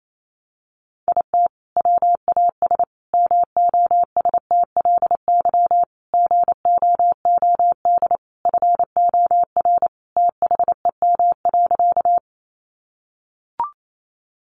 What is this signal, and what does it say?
Morse code, heard as a keyed tone, about 28 wpm, 700 Hz. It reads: IT WAS MOSTLY GOOD FOR THEM.
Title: Sentences from Top 1300 Words 28wpm